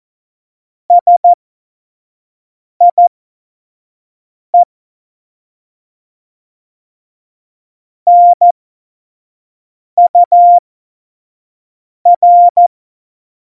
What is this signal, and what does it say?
Morse code, heard as a keyed tone, about 14 wpm, spaced as for 5 wpm, 700 Hz. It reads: SIE NUR
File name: Morsekurs / Witze / Witz 15. Januar 2019, 25 F70